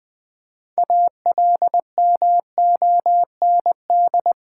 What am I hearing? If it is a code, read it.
Morse code, heard as a keyed tone, about 19 wpm, 700 Hz. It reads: ALMOND
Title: Difficult Words with Hints - 20wpm